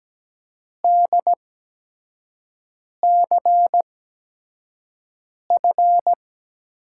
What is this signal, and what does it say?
Morse code, heard as a keyed tone, about 17 wpm, spaced as for 5 wpm, 700 Hz. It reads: DCF
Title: USA Call Signs - 8x Character Spacing - 17wpm